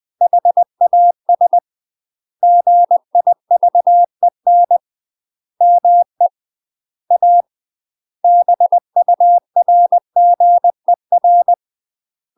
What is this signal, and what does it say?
Morse code, heard as a keyed tone, about 20 wpm, 700 Hz. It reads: HAS GIVEN ME A BURGER
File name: UK MASTER QSOs TRAINING / 20WPM / UK MASTER QSOs 0050